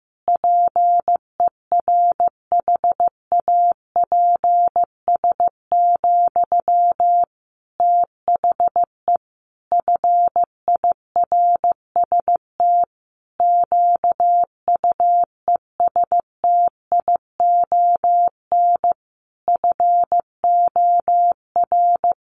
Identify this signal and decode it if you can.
Morse code, heard as a keyed tone, about 15 wpm, 700 Hz. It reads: PERHAPS, THE FIRST QUESTION FOR
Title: Sentences from Top 1500 Words 15wpm